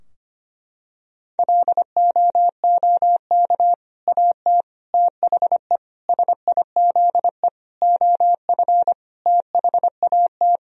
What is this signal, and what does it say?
Morse code, heard as a keyed tone, about 25 wpm, 700 Hz. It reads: LOOK AT THE SIZE OF THAT
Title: Sentences from Top 1000 Words - Review Part 3 - 25wpm